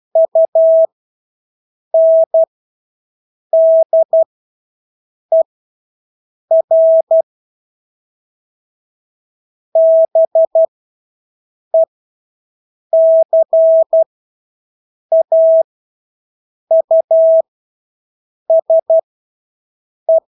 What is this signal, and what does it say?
Morse code, heard as a keyed tone, about 12 wpm, spaced as for 6 wpm, 650 Hz. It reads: UNDER BECAUSE
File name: Engelske ord 12-6 wpm